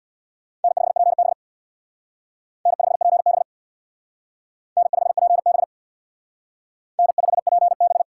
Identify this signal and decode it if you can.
Morse code, heard as a keyed tone, about 61 wpm, 700 Hz. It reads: N5PB N5PB N5PB N5PB
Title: Speed Racing - USA 1x2 Call Signs 50wpm